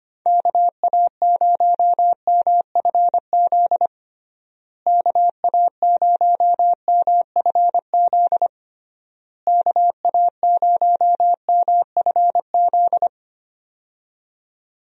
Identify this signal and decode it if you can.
Morse code, heard as a keyed tone, about 25 wpm, 700 Hz. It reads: KA0MFZ KA0MFZ KA0MFZ
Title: USA Call Signs - 6 Characters - Repeated 3x - 25wpm